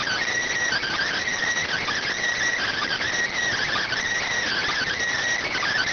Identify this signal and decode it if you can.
Emergency Alert System (SAME) tones (header bursts), heard as harsh buzzing bursts. It is heard from 0.7 to 4.9 s.